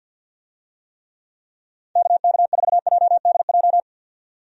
Morse code, heard as a keyed tone, about 46 wpm, 700 Hz. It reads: KK4JDJ
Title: USA Call Signs - 6 Characters - 50wpm